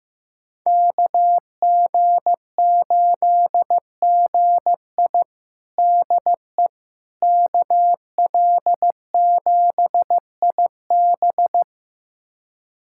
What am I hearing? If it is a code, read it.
Morse code, heard as a keyed tone, about 15 wpm, 700 Hz. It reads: KG8GI DE KL7IB
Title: USA Callsign DE Callsign 15wpm